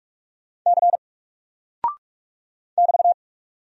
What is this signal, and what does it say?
Morse code, heard as a keyed tone, about 44 wpm, 700 Hz. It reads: C <BT>
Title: Single Character (Letter:Number:Punctuation:Prosign) Alt 45wpm